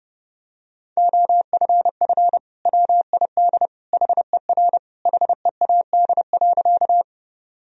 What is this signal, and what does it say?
Morse code, heard as a keyed tone, about 30 wpm, 700 Hz. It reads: OFF WID HER HEAD.
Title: Sentences from Top 300 Words and Abbreviations 30wpm